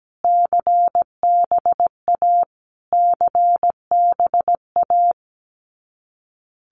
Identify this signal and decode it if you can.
Morse code, heard as a keyed tone, about 17 wpm, 700 Hz. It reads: CBA CBA
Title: Abbreviations Repeated 3x - 17wpm